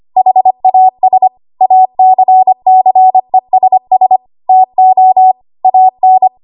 Morse code, heard as a keyed tone, about 25 wpm, 750 Hz. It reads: HAS ACCESS TO AN
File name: Archive / 25 WPM / 25WPM